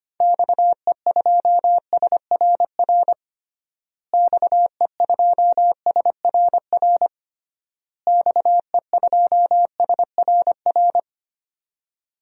Morse code, heard as a keyed tone, about 25 wpm, 700 Hz. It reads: XE2SRR XE2SRR XE2SRR
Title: DXCC-100 Call Signs - 6 Characters - Repeated 3x - 25wpm